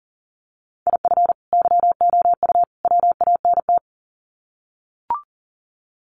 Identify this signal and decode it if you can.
Morse code, heard as a keyed tone, about 37 wpm, 700 Hz. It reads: IF YOU WANT
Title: Sets of 3 Words 40wpm